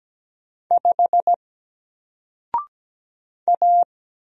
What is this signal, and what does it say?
Morse code, heard as a keyed tone, about 17 wpm, 700 Hz. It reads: 5 A
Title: Single Character (Letter:Number:Punctuation:Prosign) Alt 17wpm